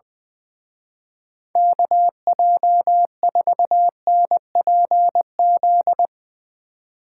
Morse code, heard as a keyed tone, about 22 wpm, 700 Hz. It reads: KJ4NPZ
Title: Speed Racing - USA 2x3 Call Signs 20wpm